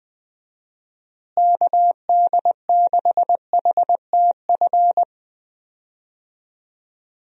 Morse code, heard as a keyed tone, about 20 wpm, 700 Hz. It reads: KD6HTF
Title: USA Call Signs - 6 Characters - Repeated 2x - 20wpm